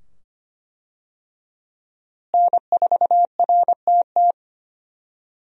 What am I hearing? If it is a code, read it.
Morse code, heard as a keyed tone, about 25 wpm, 700 Hz. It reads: N4RTT
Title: USA Call Signs - 5 Characters - Repeated 2x - 25wpm